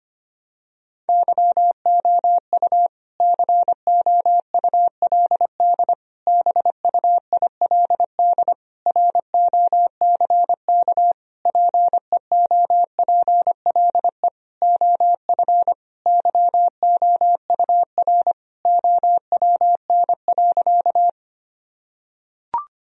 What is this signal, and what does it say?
Morse code, heard as a keyed tone, about 25 wpm, 700 Hz. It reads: YOU COULD BUILD ROCK PEOPLE OF YOUR OWN.
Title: Sentences from Top 1000 Words - Review Part 2 - 25wpm